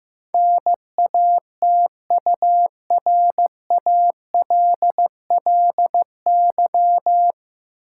Morse code, heard as a keyed tone, about 15 wpm, 700 Hz. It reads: NATURALLY
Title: Sentences from Top 1300 Words - New Word Drill Alt - 15wpm